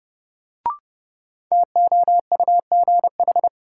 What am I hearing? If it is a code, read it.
Morse code, heard as a keyed tone, about 30 wpm, 700 Hz. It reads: TOUGH
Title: Sentences from Top 1200 Words - New Word Drill Alt - 30wpm